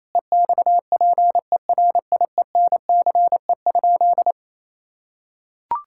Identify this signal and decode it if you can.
Morse code, heard as a keyed tone, about 27 wpm, 700 Hz. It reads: EXPERIENCE?
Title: Sentences from Top 1000 Words - Review Part 5 - 28wpm